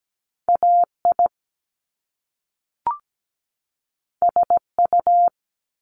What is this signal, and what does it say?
Morse code, heard as a keyed tone, about 17 wpm, 700 Hz. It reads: AI SU
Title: Top Two Letter Combinations in Words 17wpm